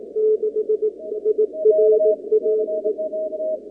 Morse code, heard as a keyed tone, about 18 wpm, 430 Hz. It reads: 6 IRAE
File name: contests / 2019-cw-wpx-kt5j-k5pi- -2330 0-37